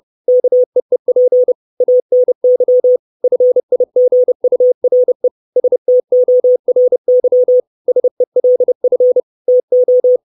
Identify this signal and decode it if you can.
Morse code, heard as a keyed tone, about 30 wpm, 500 Hz. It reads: KEEP ANY FIGURE STORY SELF TO
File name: 632 Words 30WPM